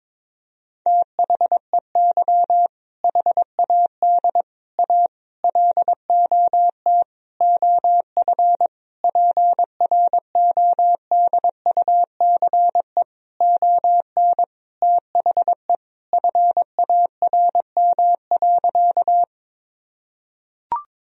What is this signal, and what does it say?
Morse code, heard as a keyed tone, about 22 wpm, 700 Hz. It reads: THEY HAD A LOT OF PRODUCE ON THE FARM.
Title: Sentences from Top 1000 Words - Review Part 3 - 22wpm